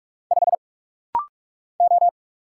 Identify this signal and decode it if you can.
Morse code, heard as a keyed tone, about 46 wpm, 700 Hz. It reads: 5 O
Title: Single Letter-Number Alt - Rapid-Fire 45wpm